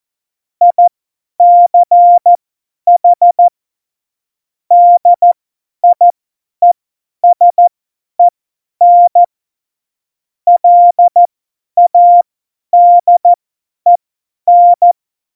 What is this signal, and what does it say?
Morse code, heard as a keyed tone, about 14 wpm, spaced as for 10 wpm, 700 Hz. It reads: ICH DIESEN LADEN
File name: Morsekurs / Witze / Witz 29. Dezember 2020, 50 F70